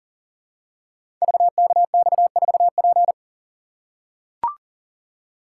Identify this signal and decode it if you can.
Morse code, heard as a keyed tone, about 38 wpm, 700 Hz. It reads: VKX4P
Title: Code Groups - 5 Letter:Numbers - 40wpm